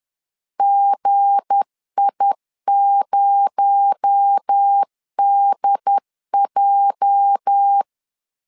Morse code, heard as a keyed tone, about 10 wpm, 800 Hz. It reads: GI0DJ
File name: LEZIONI / 8 YJ45 / 8 YJ 10wpm 2651